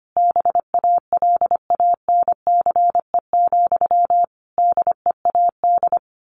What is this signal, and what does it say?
Morse code, heard as a keyed tone, about 25 wpm, 700 Hz. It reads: BALANCE, DEAD
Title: A Sound Of Thunder 25wpm